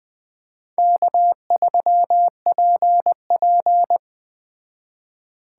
Morse code, heard as a keyed tone, about 20 wpm, 700 Hz. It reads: K3PP
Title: Speed Racing - USA 1x2 Call Signs 17wpm